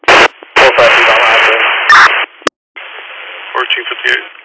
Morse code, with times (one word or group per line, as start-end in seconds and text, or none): none